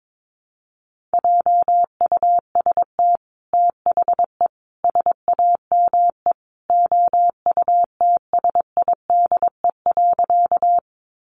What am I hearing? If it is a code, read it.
Morse code, heard as a keyed tone, about 22 wpm, 700 Hz. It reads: JUST THE SAME OUTSIDE.
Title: Sentences from Top 900 Words 22wpm